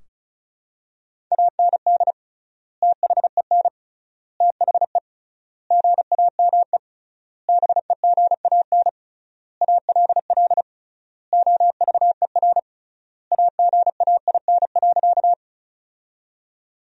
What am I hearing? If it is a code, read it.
Morse code, heard as a keyed tone, about 36 wpm, 700 Hz. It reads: AND THEN THE GAME BEGAN ALL OVER AGAIN.
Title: 3x Word-Spacing - Sentences from Top 500 Words - 35wpm